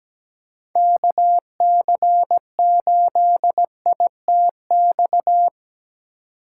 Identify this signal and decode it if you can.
Morse code, heard as a keyed tone, about 17 wpm, 700 Hz. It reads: KC8ITX
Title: USA Call Signs - 2x Character Spacing - 17wpm